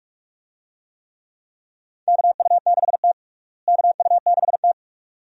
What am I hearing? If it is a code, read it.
Morse code, heard as a keyed tone, about 47 wpm, 700 Hz. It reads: KU6T KU6T
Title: USA Call Signs - 4 Characters - Repeated 2x - 45wpm